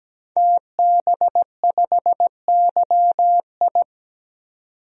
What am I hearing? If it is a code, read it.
Morse code, heard as a keyed tone, about 17 wpm, 700 Hz. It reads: TB5YI
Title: DXCC-100 Call Signs - 5 Characters - 17wpm